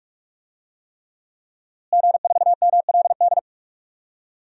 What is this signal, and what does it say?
Morse code, heard as a keyed tone, about 46 wpm, 700 Hz. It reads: G4MLD
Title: DXCC-100 Call Signs - 5 Characters - Repeated 3x - 45wpm